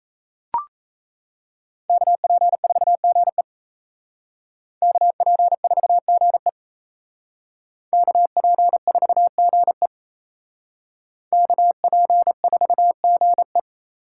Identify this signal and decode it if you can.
Morse code, heard as a keyed tone, about 35 wpm, 700 Hz. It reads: KP4GE KP4GE KP4GE KP4GE